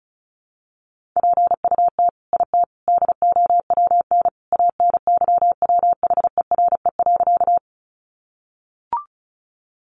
Morse code, heard as a keyed tone, about 34 wpm, 700 Hz. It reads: PUT IT DOWN ANYWHERE.
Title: Sentences from Top 1400 Words 35wpm